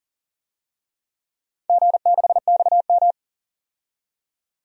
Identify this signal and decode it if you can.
Morse code, heard as a keyed tone, about 39 wpm, 700 Hz. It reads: G6XM
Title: DXCC-100 Call Signs - 4 Characters - 40wpm